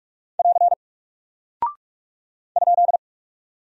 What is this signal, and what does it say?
Morse code, heard as a keyed tone, about 44 wpm, 700 Hz. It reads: <AR> ?
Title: Single Character (Letter:Number:Punctuation:Prosign) Alt 45wpm